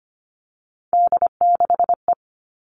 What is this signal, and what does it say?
Morse code, heard as a keyed tone, about 25 wpm, 700 Hz. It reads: D6E
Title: All Cut Numbers - 3-Digit 25wpm